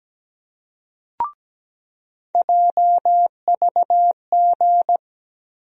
Morse code, heard as a keyed tone, about 17 wpm, 700 Hz. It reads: JVG
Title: USA Call Sign Postfixes - 3 Letters 17wpm